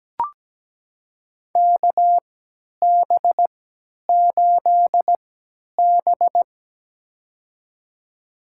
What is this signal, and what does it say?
Morse code, heard as a keyed tone, about 17 wpm, spaced as for 10 wpm, 700 Hz. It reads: KB8B
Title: USA Call Signs - 3x Character Spacing - 17wpm